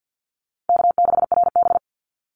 Morse code, heard as a keyed tone, about 48 wpm, 700 Hz. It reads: K6RB
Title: Call Signs Repeated 2x - 50wpm